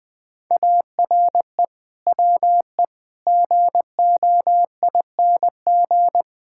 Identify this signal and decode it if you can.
Morse code, heard as a keyed tone, about 20 wpm, 700 Hz. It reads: ARE WE GOING
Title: Build A Phrase - 5-Word Phrases - 20wpm